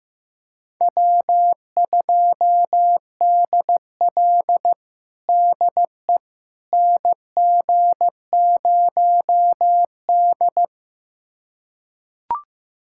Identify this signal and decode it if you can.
Morse code, heard as a keyed tone, about 15 wpm, 700 Hz. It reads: W2DL DE NG0D
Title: USA Callsign DE Callsign - 4 Characters - 15wpm